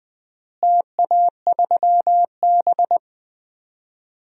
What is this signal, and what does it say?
Morse code, heard as a keyed tone, about 21 wpm, 700 Hz. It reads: TA3B
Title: DXCC-100 Call Signs - 4 Characters - Repeated 3x - 20wpm